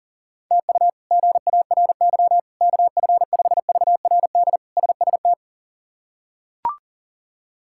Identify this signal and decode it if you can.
Morse code, heard as a keyed tone, about 37 wpm, 700 Hz. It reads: TU GARY KF5VRD SST
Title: K1USN SST QSO Practice - 40wpm